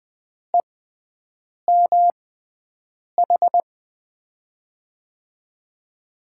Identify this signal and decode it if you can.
Morse code, heard as a keyed tone, about 20 wpm, spaced as for 7 wpm, 700 Hz. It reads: EMH
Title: USA Call Signs - 6 Characters - 6x Character Spacing - 20wpm